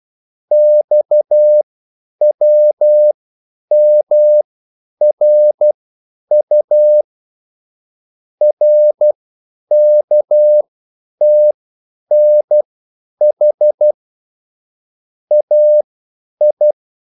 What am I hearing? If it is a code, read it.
Morse code, heard as a keyed tone, about 12 wpm, spaced as for 9 wpm, 600 Hz. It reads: XWMRU RKTNH AI
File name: QSL / CW lesson